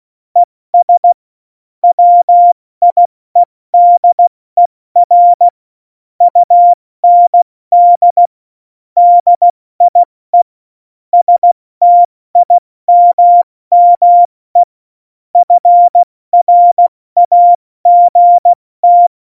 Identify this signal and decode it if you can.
Morse code, heard as a keyed tone, about 16 wpm, 700 Hz. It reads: ES WIEDER UND DIE STIMME FRAGT
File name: Morsekurs / Witze / Witz 02. Oktober 2018, 70 F80